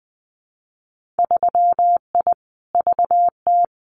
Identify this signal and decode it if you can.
Morse code, heard as a keyed tone, about 20 wpm, 700 Hz. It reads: 3I VT
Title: Winter Field Day Exchanges Encore 20wpm